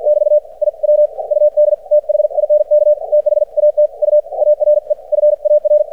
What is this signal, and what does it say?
Morse code, heard as a keyed tone, about 53 wpm, spaced as for 44 wpm, 600 Hz. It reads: <BT> I WUD THINK THAT U ARE FAR
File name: archived / CRSnet